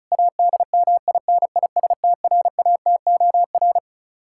Morse code, heard as a keyed tone, about 36 wpm, 700 Hz. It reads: ADMINISTRATOR
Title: K1USN SST QSO Practice 35wpm